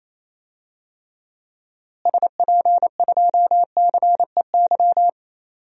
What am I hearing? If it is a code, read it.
Morse code, heard as a keyed tone, about 28 wpm, 700 Hz. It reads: SP2CEY
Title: DXCC-100 Call Signs - 6 Characters - Repeated 3x - 28wpm